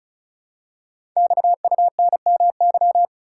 Morse code, heard as a keyed tone, about 36 wpm, 700 Hz. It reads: XUNMY
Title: Code Groups - 5 Letters - 35wpm